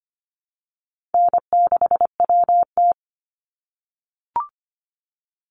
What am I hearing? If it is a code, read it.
Morse code, heard as a keyed tone, about 25 wpm, 700 Hz. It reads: N6WT
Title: NA SOTA Chaser Callsigns 25wpm